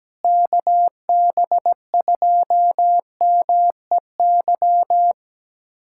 Morse code, heard as a keyed tone, about 17 wpm, 700 Hz. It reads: KB2MEY
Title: USA Callsign DE Callsign - 6 Characters - 17wpm